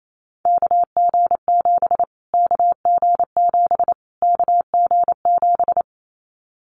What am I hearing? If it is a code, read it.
Morse code, heard as a keyed tone, about 28 wpm, 700 Hz. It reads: KG7 KG7 KG7
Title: Callsign Prefixes for Top 25 DXCC - Repeated 3x - 28wpm